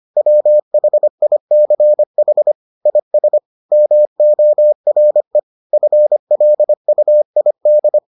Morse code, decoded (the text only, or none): WHICH IS MORE FLUID